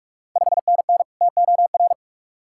Morse code, heard as a keyed tone, about 44 wpm, 690 Hz. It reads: 5NN TOR
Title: Ontario QSO Party Exchange 45wpm